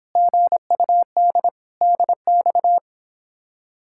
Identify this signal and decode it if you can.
Morse code, heard as a keyed tone, about 31 wpm, 700 Hz. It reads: GUD DX